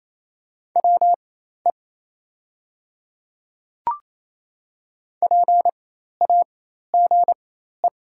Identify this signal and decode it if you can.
Morse code, heard as a keyed tone, about 28 wpm, spaced as for 13 wpm, 700 Hz. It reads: WE PAGE
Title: 4x Character Spacing - Top 500 Words - 28wpm